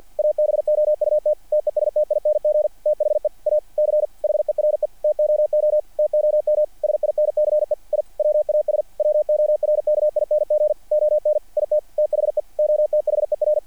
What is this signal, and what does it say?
Morse code, heard as a keyed tone, about 48 wpm, 660 Hz. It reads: ABOUT TESTING THE A C HERE TOO TOM SINCE I WAS WORKING ON IT THE OTHER
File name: archived / CRSnet